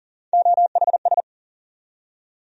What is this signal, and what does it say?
Morse code, heard as a keyed tone, about 40 wpm, 700 Hz. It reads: OHS
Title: USA Call Sign Postfixes - 3 Letters 40wpm